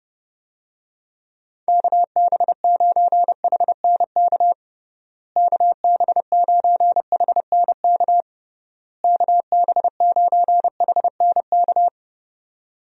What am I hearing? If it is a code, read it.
Morse code, heard as a keyed tone, about 29 wpm, 700 Hz. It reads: KB9HNK KB9HNK KB9HNK
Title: USA Call Signs - 6 Characters - Repeated 3x - 30wpm